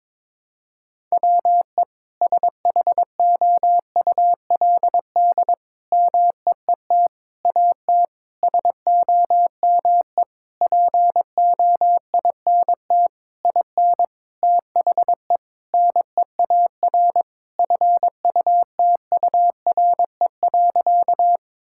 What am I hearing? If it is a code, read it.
Morse code, heard as a keyed tone, about 22 wpm, 700 Hz. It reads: WE SHOULD MEET AT SOME POINT IN THE NEAR FUTURE.
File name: Sentences with 7-Word Phrases - 22wpm